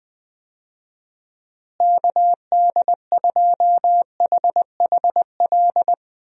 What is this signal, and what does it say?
Morse code, heard as a keyed tone, about 20 wpm, 700 Hz. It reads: KD2HHL
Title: USA Call Signs - 6 Characters - Repeated 3x - 20wpm